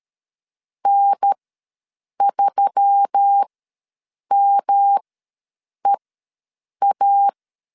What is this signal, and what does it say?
Morse code, heard as a keyed tone, about 13 wpm, spaced as for 7 wpm, 800 Hz. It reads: N3MEA
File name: LEZIONI / 7 GF9Z / 7 GF9 ws 12wpm 3344